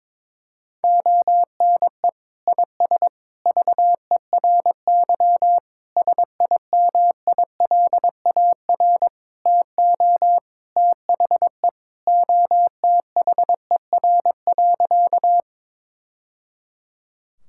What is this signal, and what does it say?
Morse code, heard as a keyed tone, about 22 wpm, 700 Hz. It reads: ONE IS VERY SIMILAR TO THE OTHER.
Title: Sentences from Top 1000 Words - Review Part 5 - 22wpm